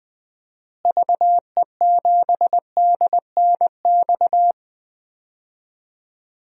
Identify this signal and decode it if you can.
Morse code, heard as a keyed tone, about 20 wpm, 700 Hz. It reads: VE7DNX
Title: DXCC-100 Call Signs - 6 Characters - Repeated 2x - 20wpm